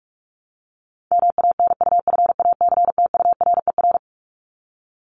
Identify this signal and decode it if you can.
Morse code, heard as a keyed tone, about 44 wpm, 700 Hz. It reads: MANUFACTURER
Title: Top 1751-2000 Words 45wpm